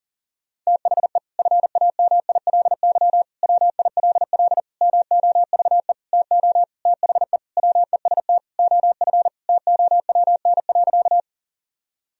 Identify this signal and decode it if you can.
Morse code, heard as a keyed tone, about 40 wpm, 700 Hz. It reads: THE FAMILY WILL MOVE TO THE WEST OF TOWN.
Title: Sentences from Top 1000 Words - Review Part 2 - 40wpm